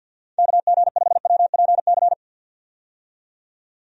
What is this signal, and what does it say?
Morse code, heard as a keyed tone, about 46 wpm, 700 Hz. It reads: KG5WPC
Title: USA Call Signs - 6 Characters - 50wpm